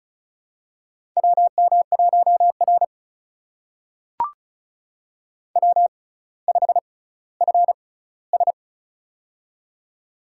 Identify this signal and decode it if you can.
Morse code, heard as a keyed tone, about 36 wpm, spaced as for 18 wpm, 700 Hz. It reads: WM1R W5FS
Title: USA Call Signs - 4 Characters - 6x Character Spacing - 35wpm